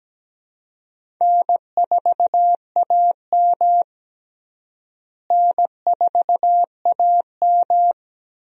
Morse code, heard as a keyed tone, about 17 wpm, 700 Hz. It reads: N4AM N4AM
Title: USA Call Signs - 4 Characters - Repeated 3x - 17wpm